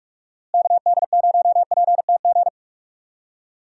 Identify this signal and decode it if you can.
Morse code, heard as a keyed tone, about 46 wpm, 700 Hz. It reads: KD0PTG
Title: USA Call Signs - 6 Characters - Repeated 3x - 45wpm